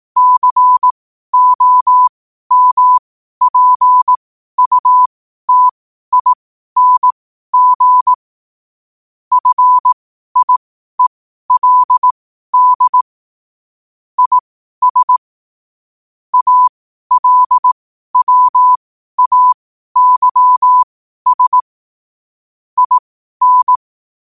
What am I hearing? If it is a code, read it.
Morse code, heard as a keyed tone, about 18 wpm, spaced as for 12 wpm, 1000 Hz. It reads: COMPUTING FIELD IS ALWAYS IN
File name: qotdfiles / 12wpm